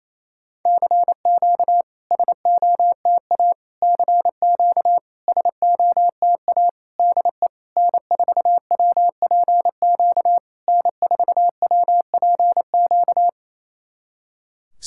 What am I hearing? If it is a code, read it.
Morse code, heard as a keyed tone, about 28 wpm, 700 Hz. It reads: CQ SOTA CQ SOTA DE N4WPQ N4WPQ
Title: SOTA QSO Practice 28wpm